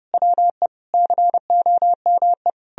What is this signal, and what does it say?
Morse code, heard as a keyed tone, about 40 wpm, 700 Hz. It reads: WE COME